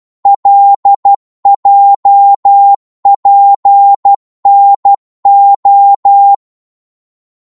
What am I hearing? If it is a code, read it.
Morse code, heard as a keyed tone, about 12 wpm, 800 Hz. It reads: LJPNO
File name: Morse / ltr-12wpm